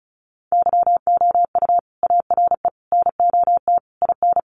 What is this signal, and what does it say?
Morse code, heard as a keyed tone, about 34 wpm, 700 Hz. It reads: YOU ARE NOT IN
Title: Sentences from Top 1300 Words 35wpm